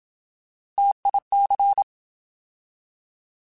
Morse code, heard as a keyed tone, about 27 wpm, 800 Hz. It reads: TIC
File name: Alphabetical / Alphabetical mp3s / Ts @25